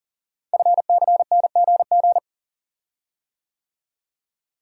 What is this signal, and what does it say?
Morse code, heard as a keyed tone, about 42 wpm, 700 Hz. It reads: FCNGG
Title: Code Groups - 5 Letters - 40wpm